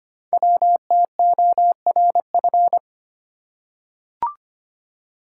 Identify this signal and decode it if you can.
Morse code, heard as a keyed tone, about 25 wpm, 700 Hz. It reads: WTORF
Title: Code Groups - 5 Letter:Numbers - 25wpm